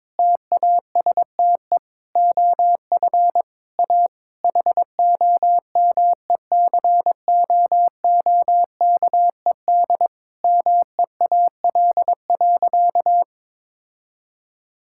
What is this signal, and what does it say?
Morse code, heard as a keyed tone, about 22 wpm, 700 Hz. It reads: TASTE OF A HOMECOOKED MEAL.
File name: Difficult Sentences with Hints - 22wpm